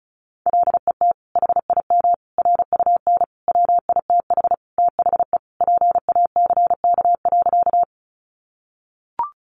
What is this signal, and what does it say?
Morse code, heard as a keyed tone, about 34 wpm, 700 Hz. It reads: LET HIM RUN WITH THE PACK.
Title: Sentences from Top 1300 Words 35wpm